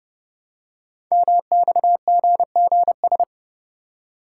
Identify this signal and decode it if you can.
Morse code, heard as a keyed tone, about 30 wpm, 690 Hz. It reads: MXGGS
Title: Code Groups - 5 Letters - 30wpm